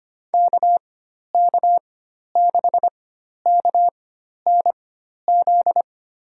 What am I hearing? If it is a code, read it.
Morse code, heard as a keyed tone, about 25 wpm, spaced as for 12 wpm, 700 Hz. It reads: KK6KNZ
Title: USA Call Signs - 6 Characters - 4x Character Spacing - 25wpm